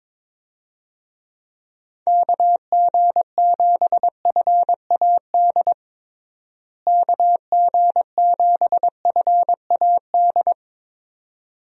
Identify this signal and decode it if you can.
Morse code, heard as a keyed tone, about 22 wpm, 700 Hz. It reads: KG7FAD KG7FAD
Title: USA Call Signs - 6 Characters - Repeated 2x - 22wpm